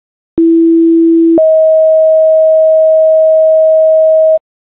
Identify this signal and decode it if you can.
Two-tone page, heard as a one-second tone, then a three-second tone, about 330 then 640 Hz. It is heard from 0.4 to 4.4 s.